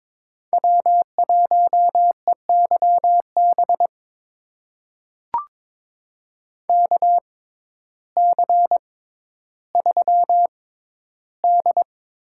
Morse code, heard as a keyed tone, about 21 wpm, spaced as for 12 wpm, 700 Hz. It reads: W1EYB KC3D
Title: USA Call Signs - 5 Characters - 6x Character Spacing - 22wpm